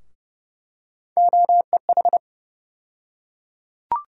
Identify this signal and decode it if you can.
Morse code, heard as a keyed tone, about 30 wpm, 700 Hz. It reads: OEH
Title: USA Call Sign Postfixes - 3 Letters 30wpm